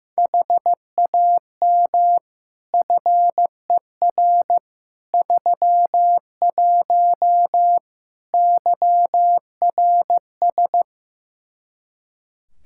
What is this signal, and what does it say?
Morse code, heard as a keyed tone, about 16 wpm, 700 Hz. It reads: HAM FER 31 YRS
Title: K1USN SST QSO Practice 15wpm